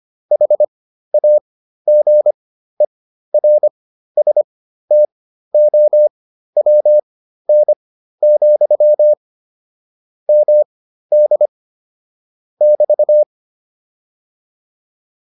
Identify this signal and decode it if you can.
Morse code, heard as a keyed tone, about 25 wpm, spaced as for 13 wpm, 600 Hz. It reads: HAGERSTOWN, MD <BT>